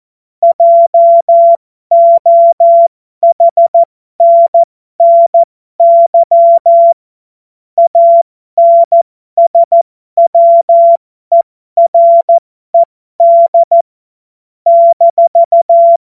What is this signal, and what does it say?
Morse code, heard as a keyed tone, about 14 wpm, spaced as for 12 wpm, 680 Hz. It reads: JOHNNY ANSWERED -
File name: Morsekurs / Witze